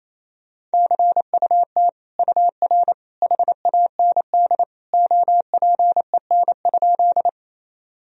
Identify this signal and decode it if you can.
Morse code, heard as a keyed tone, about 28 wpm, 700 Hz. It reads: CUT UR HAND OPEN?
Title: Sentences from Top 300 Words and Abbreviations 28wpm